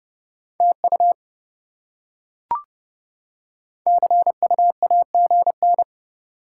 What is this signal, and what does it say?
Morse code, heard as a keyed tone, about 28 wpm, 700 Hz. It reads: TU CU AGN